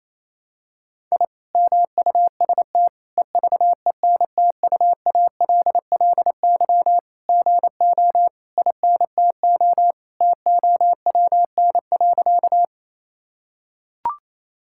Morse code, heard as a keyed tone, about 28 wpm, 700 Hz. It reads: I MUST EVENTUALLY GO INTO TOWN.
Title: Sentences from Top 1000 Words - Review Part 6 - 28wpm